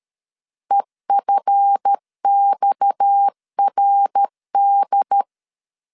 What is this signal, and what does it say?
Morse code, heard as a keyed tone, about 13 wpm, 800 Hz. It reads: EFXRD
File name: LEZIONI / 9 2 12wpm 2301